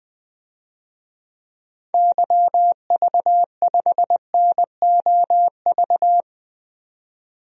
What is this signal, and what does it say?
Morse code, heard as a keyed tone, about 21 wpm, 700 Hz. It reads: YV5NOV
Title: DXCC-100 Call Signs - 6 Characters - 20wpm